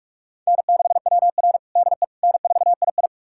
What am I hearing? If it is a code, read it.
Morse code, heard as a keyed tone, about 45 wpm, 700 Hz. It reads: N6WR DE N4II
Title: USA Callsign DE Callsign - 4 Characters - 45wpm